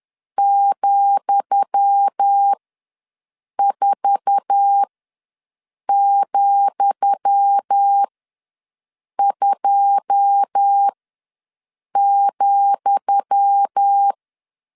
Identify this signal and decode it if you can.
Morse code, heard as a keyed tone, about 10 wpm, spaced as for 6 wpm, 800 Hz. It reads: ,4,2,
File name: LEZIONI / 10 SIGNS / 10D COMA ws 10wpm 4314